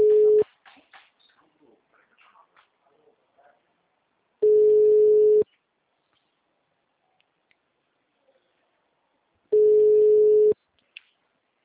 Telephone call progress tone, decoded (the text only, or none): ringback tone